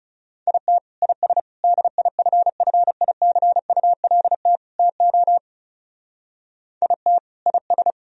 Morse code, heard as a keyed tone, about 38 wpm, 700 Hz. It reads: IT IS DIFFICULT TO IT IS